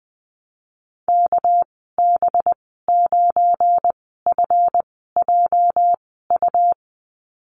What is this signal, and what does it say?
Morse code, heard as a keyed tone, about 20 wpm, spaced as for 15 wpm, 700 Hz. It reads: KB9FJU
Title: 238 Lesson 38g - Letter q - Call Signs 20wpm